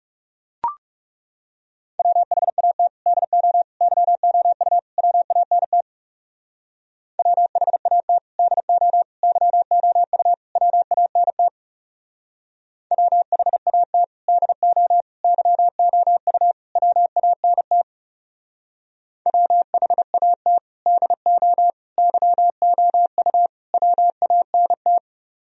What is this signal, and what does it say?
Morse code, heard as a keyed tone, about 39 wpm, 700 Hz. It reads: WHAT DO YOU WANT WHAT DO YOU WANT WHAT DO YOU WANT WHAT DO YOU WANT